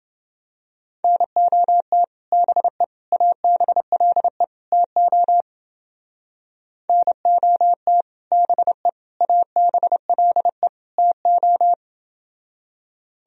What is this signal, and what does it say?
Morse code, heard as a keyed tone, about 30 wpm, 700 Hz. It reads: NOT BE ABLE TO NOT BE ABLE TO